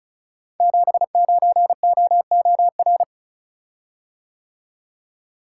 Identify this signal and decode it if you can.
Morse code, heard as a keyed tone, about 36 wpm, 700 Hz. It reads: 79OOR
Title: Code Groups - 5 Letter:Numbers - 35wpm